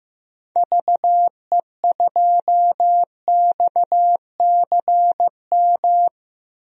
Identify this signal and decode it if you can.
Morse code, heard as a keyed tone, about 15 wpm, 700 Hz. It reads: VE2XCM
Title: DXCC-100 Call Signs - 6 Characters - Repeated 2x - 15wpm